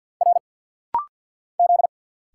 Morse code, heard as a keyed tone, about 50 wpm, 690 Hz. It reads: R Z
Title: Single Letters - Rapid-Fire 50wpm